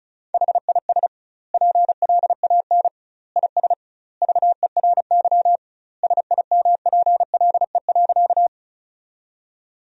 Morse code, heard as a keyed tone, about 36 wpm, 700 Hz. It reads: HIS PLAN IS VERY SIMPLE.
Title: 2x Word-Spacing - Sentences from Top 500 Words - 35wpm